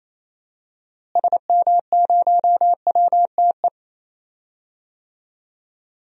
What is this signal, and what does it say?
Morse code, heard as a keyed tone, about 27 wpm, 700 Hz. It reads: SM0WTE
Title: DXCC-100 Call Signs - 6 Characters - Repeated 2x - 28wpm